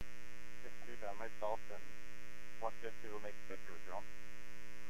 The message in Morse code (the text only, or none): none